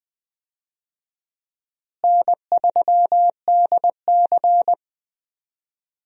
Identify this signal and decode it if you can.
Morse code, heard as a keyed tone, about 21 wpm, 700 Hz. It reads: N3DC
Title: USA Call Signs - 4 Characters - Repeated 2x - 20wpm